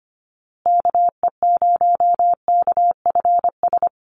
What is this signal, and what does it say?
Morse code, heard as a keyed tone, about 25 wpm, 700 Hz. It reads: KE0KFS
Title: USA Call Signs Repeated 2x 25wpm